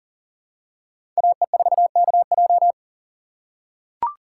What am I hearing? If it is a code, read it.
Morse code, heard as a keyed tone, about 40 wpm, 700 Hz. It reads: AE4KJ
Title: USA Call Signs - 5 Characters - 2x Character Spacing - 40wpm